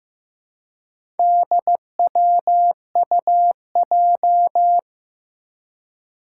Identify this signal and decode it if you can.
Morse code, heard as a keyed tone, about 15 wpm, 700 Hz. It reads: DWUJ
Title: Four Challenging Letter-Number Combinations 15wpm